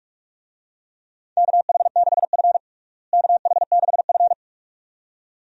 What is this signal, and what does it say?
Morse code, heard as a keyed tone, about 46 wpm, 700 Hz. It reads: KH6F KH6F
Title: USA Call Signs - 4 Characters - Repeated 2x - 45wpm